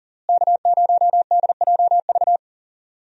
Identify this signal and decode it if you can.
Morse code, heard as a keyed tone, about 40 wpm, 700 Hz. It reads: K0DJV
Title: USA Call Signs - 5 Characters - Repeated 2x - 40wpm